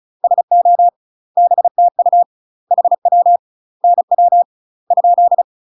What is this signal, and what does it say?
Morse code, heard as a keyed tone, about 34 wpm, 700 Hz. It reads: SO BTU HW NW ?
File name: UK MASTER QSOs TRAINING / 35WPM / UK MASTER QSOs 0023